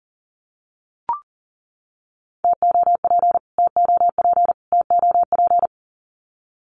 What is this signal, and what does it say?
Morse code, heard as a keyed tone, about 40 wpm, 700 Hz. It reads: TOP TOP TOP
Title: Top 500 Words Repeated 40wpm